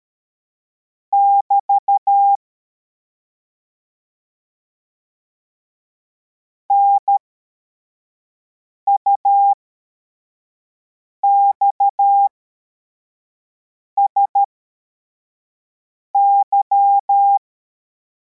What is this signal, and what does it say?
Morse code, heard as a keyed tone, about 13 wpm, spaced as for 4 wpm, 800 Hz. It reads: <BT> NUXSY